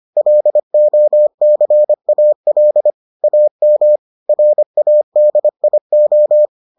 Morse code, heard as a keyed tone, about 25 wpm, 600 Hz. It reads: LOCAL AM RADIO